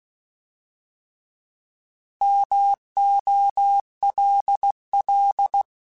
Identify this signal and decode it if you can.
Morse code, heard as a keyed tone, about 16 wpm, 800 Hz. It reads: MOLL
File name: Alphabetical / Alphabetical mp3s / Ms @15